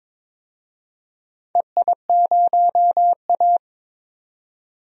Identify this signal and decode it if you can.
Morse code, heard as a keyed tone, about 22 wpm, 700 Hz. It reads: EI0A
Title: DXCC-100 Call Signs - 4 Characters - 22wpm